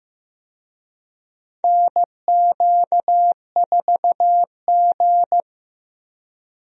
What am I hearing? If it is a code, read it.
Morse code, heard as a keyed tone, about 16 wpm, 700 Hz. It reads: NQ4G
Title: USA Call Signs - 4 Characters - 15wpm